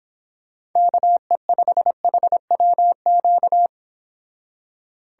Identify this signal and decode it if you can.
Morse code, heard as a keyed tone, about 26 wpm, 700 Hz. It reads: KE5HWQ
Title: Speed Racing - USA 2x3 Call Signs 22wpm